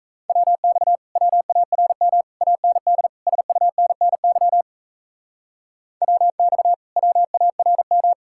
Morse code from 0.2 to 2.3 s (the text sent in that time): WX WARM